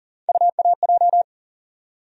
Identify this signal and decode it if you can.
Morse code, heard as a keyed tone, about 40 wpm, 700 Hz. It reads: UAJ
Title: USA Call Sign Postfixes - 3 Letters 40wpm